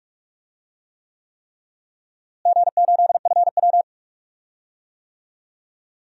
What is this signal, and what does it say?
Morse code, heard as a keyed tone, about 46 wpm, 700 Hz. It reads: G8FW
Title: DXCC-100 Call Signs - 4 Characters - Repeated 3x - 45wpm